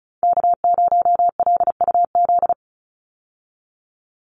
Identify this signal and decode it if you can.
Morse code, heard as a keyed tone, about 34 wpm, 700 Hz. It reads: K0LUZ
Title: Call Signs - 35wpm